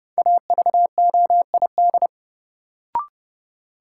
Morse code, heard as a keyed tone, about 30 wpm, 700 Hz. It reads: AVOID
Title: Sentences from Top 1000 Words - Review Part 6 - New Word Drill - 30wpm